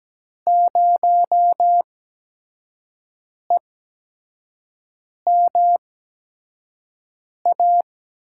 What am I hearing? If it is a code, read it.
Morse code, heard as a keyed tone, about 17 wpm, spaced as for 5 wpm, 700 Hz. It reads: 0EMA
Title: USA Call Signs - 5 Characters - 8x Character Spacing - 17wpm